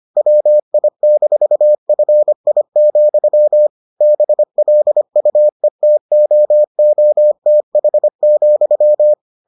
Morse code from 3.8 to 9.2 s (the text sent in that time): BLUETOOTH,